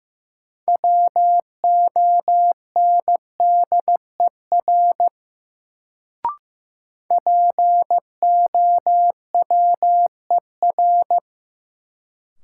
Morse code, heard as a keyed tone, about 15 wpm, 700 Hz. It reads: WONDER POWER
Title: Sentences from Top 500 Words - New Word Drill - 15wpm